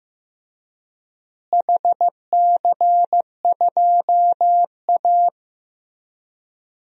Morse code, heard as a keyed tone, about 16 wpm, 700 Hz. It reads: HC2A
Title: DXCC-100 Call Signs - 4 Characters - Repeated 3x - 15wpm